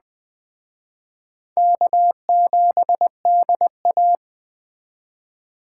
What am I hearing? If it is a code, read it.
Morse code, heard as a keyed tone, about 20 wpm, 700 Hz. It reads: K7DA
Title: Speed Racing - USA 1x2 Call Signs 17wpm